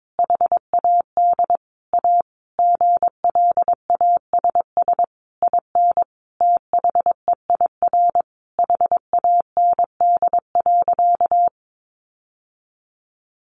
Morse code, heard as a keyed tone, about 22 wpm, 700 Hz. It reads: HAD A GLASS IN THEIR HAND.
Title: Sentences from Top 1300 Words 22wpm